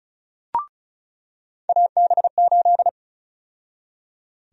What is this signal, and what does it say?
Morse code, heard as a keyed tone, about 34 wpm, 700 Hz. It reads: AB8
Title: USA Call Sign Prefixes - 35wpm